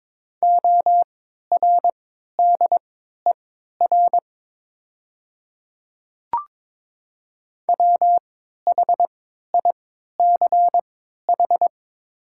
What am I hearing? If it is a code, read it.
Morse code, heard as a keyed tone, about 22 wpm, spaced as for 12 wpm, 700 Hz. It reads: ORDER WHICH
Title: 3x Character Spacing - Top 500 Words - 22wpm